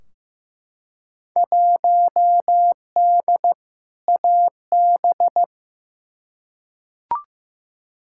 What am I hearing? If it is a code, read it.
Morse code, heard as a keyed tone, about 15 wpm, 700 Hz. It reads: 1D AB
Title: Field Day Exchanges Encore - 15wpm